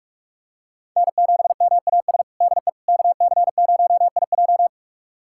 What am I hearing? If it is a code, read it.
Morse code, heard as a keyed tone, about 47 wpm, 700 Hz. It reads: N7MAS DE KC0IJ
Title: USA Callsign DE Callsign - 5 Characters - 45wpm